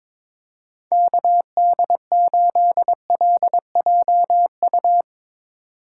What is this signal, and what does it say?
Morse code, heard as a keyed tone, about 22 wpm, 700 Hz. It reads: KD8LJU
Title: USA Call Signs - 6 Characters - 3x Character Spacing - 22wpm